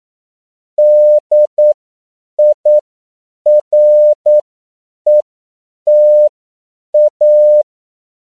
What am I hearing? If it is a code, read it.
Morse code, heard as a keyed tone, about 9 wpm, spaced as for 7 wpm, 600 Hz. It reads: DIRETA
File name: CW / 10PPM